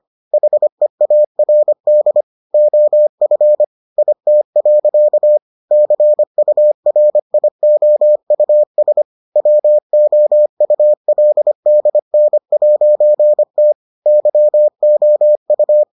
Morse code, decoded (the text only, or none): HEARD OF IT. CURIOUS WOULDN'T YOU